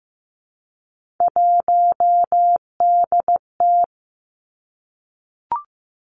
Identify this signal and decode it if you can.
Morse code, heard as a keyed tone, about 15 wpm, 700 Hz. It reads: 1DT
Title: Three Challenging Letter-Number Combinations 15wpm